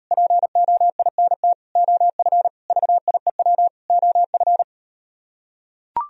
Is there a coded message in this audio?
Yes: Morse code from 0.1 to 4.6 s, heard as a keyed tone, about 38 wpm, 700 Hz. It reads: POINT OF VIEW OF